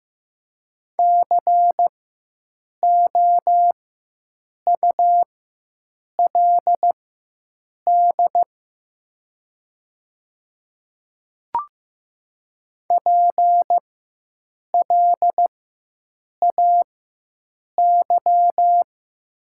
Morse code, heard as a keyed tone, about 15 wpm, spaced as for 7 wpm, 700 Hz. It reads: COULD PLAY
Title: 4x Character Spacing - Top 200 Words - 15wpm